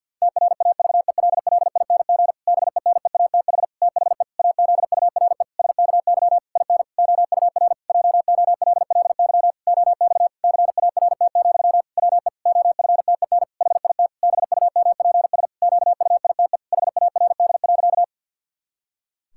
Morse code, heard as a keyed tone, about 52 wpm, 700 Hz. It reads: TRAVELLING BENEATH THE AZURE SKY IN OUR JOLLY OX CART, WE OFTEN HIT BUMPS QUITE HARD.
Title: Pangram Challenge v2 - 50wpm